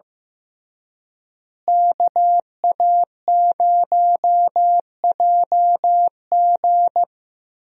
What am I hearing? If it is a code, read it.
Morse code, heard as a keyed tone, about 15 wpm, 700 Hz. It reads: KA0JG
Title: USA Call Signs - 5 Characters - Repeated 3x - 15wpm